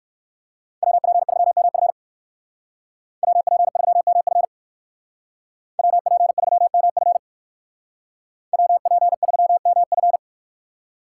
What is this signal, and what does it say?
Morse code, heard as a keyed tone, about 64 wpm, 700 Hz. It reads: WP3MF WP3MF WP3MF WP3MF